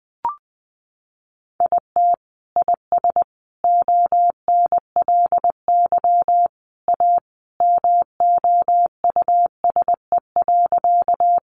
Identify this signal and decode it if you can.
Morse code, heard as a keyed tone, about 20 wpm, 700 Hz. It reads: IT IS ONLY A MOUSE.
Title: Sentences from Top 700 Words 20wpm